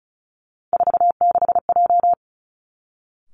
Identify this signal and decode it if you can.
Morse code, heard as a keyed tone, about 36 wpm, 700 Hz. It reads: 46J
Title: Three Challenging Letter-Number Combinations 35wpm